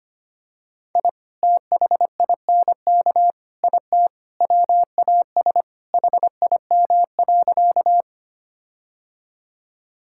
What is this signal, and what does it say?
Morse code, heard as a keyed tone, about 25 wpm, 700 Hz. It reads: I THINK IT WAS HIM.
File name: practice / 25wpm / Sentences with 4-Word Phrases - 25wpm